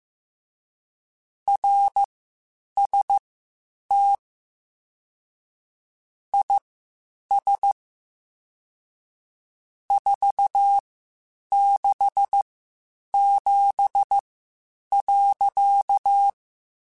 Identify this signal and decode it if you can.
Morse code, heard as a keyed tone, about 15 wpm, spaced as for 8 wpm, 800 Hz. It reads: RST IS 467.
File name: (spaced) / QSOs Spaced-mp3s / WA7QOX 2 @13 sp